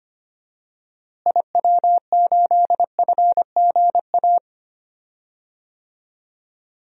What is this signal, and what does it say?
Morse code, heard as a keyed tone, about 25 wpm, 700 Hz. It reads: IW8FGA
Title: DXCC-100 Call Signs - 6 Characters - Repeated 3x - 25wpm